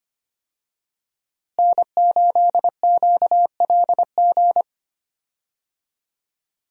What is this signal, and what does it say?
Morse code, heard as a keyed tone, about 25 wpm, 700 Hz. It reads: N8QLG
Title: USA Call Signs - 5 Characters - Repeated 2x - 25wpm